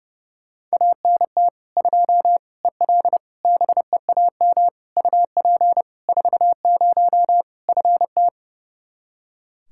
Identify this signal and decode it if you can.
Morse code, heard as a keyed tone, about 30 wpm, 700 Hz. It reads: ANT 2 EL BEAM UP 40 FT
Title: Practice QSO - Exchange 2 - Info - 30wpm